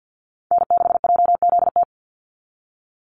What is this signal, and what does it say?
Morse code, heard as a keyed tone, about 50 wpm, 700 Hz. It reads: N6JZT
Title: NA SOTA Chaser Callsigns 50wpm